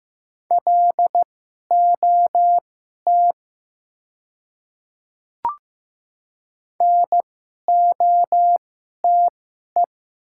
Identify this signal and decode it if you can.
Morse code, heard as a keyed tone, about 15 wpm, spaced as for 11 wpm, 700 Hz. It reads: LOT NOTE
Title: 2x Character Spacing - Top 500 Words - 15wpm